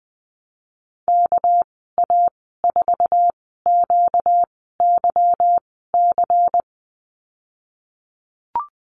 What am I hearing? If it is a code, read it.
Morse code, heard as a keyed tone, about 20 wpm, spaced as for 15 wpm, 700 Hz. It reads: KA4QYC
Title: 245 Lesson 39g - Letter z - Call Signs 20wpm